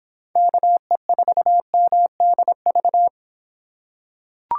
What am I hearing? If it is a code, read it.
Morse code, heard as a keyed tone, about 25 wpm, 700 Hz. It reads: KE4MDV